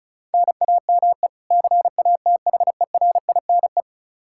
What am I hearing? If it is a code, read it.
Morse code, heard as a keyed tone, about 34 wpm, 700 Hz. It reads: NAME CATHERINE
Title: Practice QSO - Exchange 1 - Beginning - 35wpm